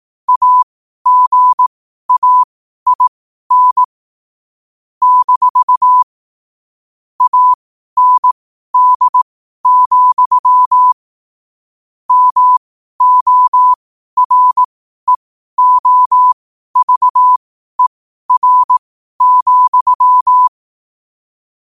Morse code, heard as a keyed tone, about 18 wpm, spaced as for 12 wpm, 1000 Hz. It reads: AGAIN - AND, MOREOVER,
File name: qotdfiles / 12wpm